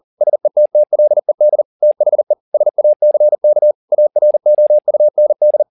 Morse code, heard as a keyed tone, about 40 wpm, 600 Hz. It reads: SETTLED THE SACK AROUND
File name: the old man and the sea 2 40wpm